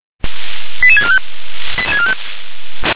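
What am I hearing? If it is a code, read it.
Five-tone selective call (ZVEI), heard as quick runs of five tones, tones of about 80 ms. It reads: 88645
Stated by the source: ZVEI five-tone selective call received on an SDR.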